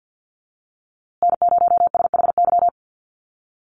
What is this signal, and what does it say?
Morse code, heard as a keyed tone, about 50 wpm, 700 Hz. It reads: N0SHY
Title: USA Call Signs Repeated 2x 50wpm